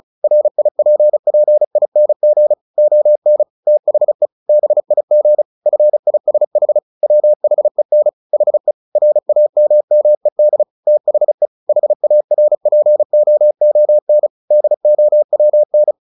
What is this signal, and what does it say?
Morse code, heard as a keyed tone, about 34 wpm, 600 Hz. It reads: RIPPING ON THE BIG FISH WHEN HE RAMMED THE HARPOON DOWN
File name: the old man and the sea 2 35wpm